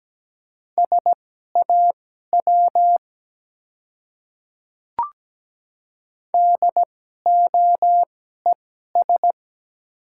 Morse code, heard as a keyed tone, about 17 wpm, spaced as for 12 wpm, 700 Hz. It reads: SAW DOES
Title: 2x Character Spacing - Top 300 Words - 17wpm